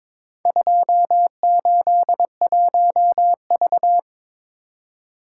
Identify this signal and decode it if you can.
Morse code, heard as a keyed tone, about 22 wpm, 700 Hz. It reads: 281V
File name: practice / 22wpm / Four Letter-Number Combinations - 22wpm